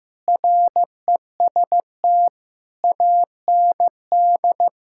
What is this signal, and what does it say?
Morse code, heard as a keyed tone, about 15 wpm, 700 Hz. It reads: REST AND
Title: Sentences from Top 1600 Words 15wpm